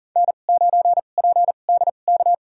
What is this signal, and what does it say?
Morse code, heard as a keyed tone, about 40 wpm, spaced as for 29 wpm, 700 Hz. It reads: N9PDK
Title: USA Call Signs - 5 Characters - 2x Character Spacing - 40wpm